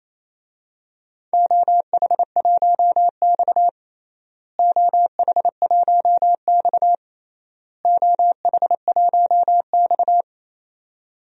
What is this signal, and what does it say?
Morse code, heard as a keyed tone, about 28 wpm, 700 Hz. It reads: OH1X OH1X OH1X
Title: DXCC-100 Call Signs - 4 Characters - Repeated 3x - 28wpm